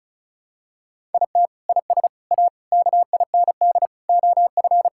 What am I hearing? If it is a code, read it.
Morse code, heard as a keyed tone, about 36 wpm, 700 Hz. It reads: IT IS A KIND OF
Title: Sentences from Top 1000 Words - Review Part 5 - 35wpm